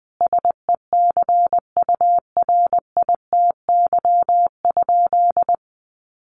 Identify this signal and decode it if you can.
Morse code, heard as a keyed tone, about 20 wpm, 700 Hz. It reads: SECURITY?
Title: Sentences from Top 1400 Words 20wpm